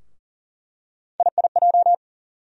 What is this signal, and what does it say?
Morse code, heard as a keyed tone, about 40 wpm, 700 Hz. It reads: IIJ
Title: USA Call Sign Postfixes - 3 Letters 40wpm